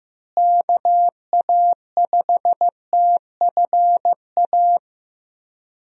Morse code, heard as a keyed tone, about 16 wpm, 700 Hz. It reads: KA5TFA
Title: USA Call Signs - 6 Characters - Repeated 2x - 15wpm